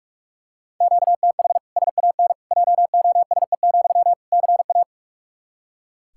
Morse code, heard as a keyed tone, about 42 wpm, 700 Hz. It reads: QTH SAN JOSE, CA
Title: Practice QSO - Exchange 1 - Beginning - 45wpm